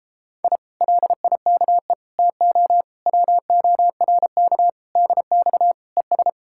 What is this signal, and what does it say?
Morse code, heard as a keyed tone, about 31 wpm, 700 Hz. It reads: I LIKE TO WORK DX ES